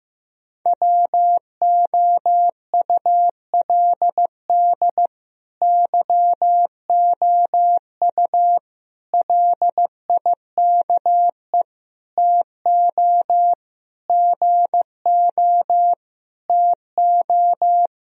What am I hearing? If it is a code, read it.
Morse code, heard as a keyed tone, about 15 wpm, 700 Hz. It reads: WOULD YOU LIKE TO GO TO
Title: Sentences from Top 1000 Words - Review Part 1 - 15wpm